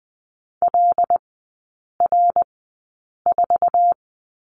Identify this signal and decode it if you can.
Morse code, heard as a keyed tone, about 20 wpm, spaced as for 8 wpm, 700 Hz. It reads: LR4
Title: 074 Lesson 13e - Letter L - Three Characters 20wpm